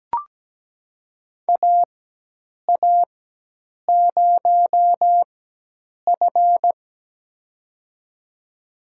Morse code, heard as a keyed tone, about 17 wpm, spaced as for 8 wpm, 700 Hz. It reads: AA0F
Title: USA Call Signs - 4 Characters - 4x Character Spacing - 17wpm